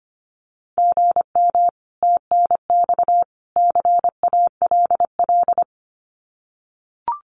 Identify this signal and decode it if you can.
Morse code, heard as a keyed tone, about 25 wpm, 700 Hz. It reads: GM TNX CALL
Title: Common QSO Phrases 25wpm